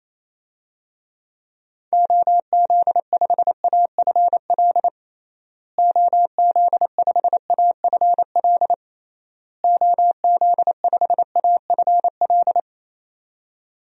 Morse code, heard as a keyed tone, about 28 wpm, 700 Hz. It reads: OZ5AFL OZ5AFL OZ5AFL
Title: DXCC-100 Call Signs - 6 Characters - Repeated 3x - 28wpm